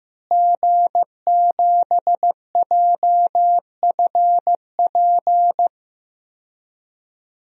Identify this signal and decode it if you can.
Morse code, heard as a keyed tone, about 15 wpm, 700 Hz. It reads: G7JFP
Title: DXCC-100 Call Signs - 5 Characters - Repeated 2x - 15wpm